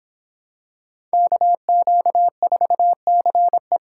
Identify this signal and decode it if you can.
Morse code, heard as a keyed tone, about 26 wpm, 700 Hz. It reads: KQ4CE